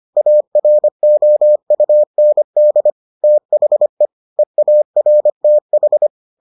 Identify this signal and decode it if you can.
Morse code, heard as a keyed tone, about 25 wpm, 600 Hz. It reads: AROUND THE EARTH